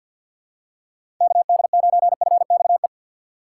Morse code, heard as a keyed tone, about 53 wpm, 700 Hz. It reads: KD9FXE
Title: USA Call Signs - 6 Characters - 6x Character Spacing - 50wpm